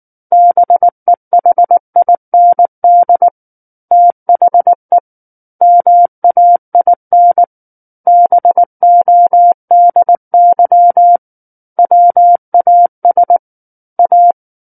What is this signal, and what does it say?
Morse code, heard as a keyed tone, about 19 wpm, 700 Hz. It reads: BEHIND THE MAIN BODY WAS A
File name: Book One / 10-Chapter 10 - 19 WPM